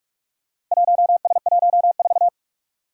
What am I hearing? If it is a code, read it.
Morse code, heard as a keyed tone, about 44 wpm, 700 Hz. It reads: 1S14
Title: Four Challenging Letter-Number Combinations 45wpm